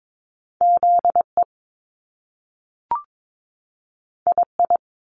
Morse code, heard as a keyed tone, about 22 wpm, 700 Hz. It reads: ZE II
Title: Top Two Letter Combinations in Words 22wpm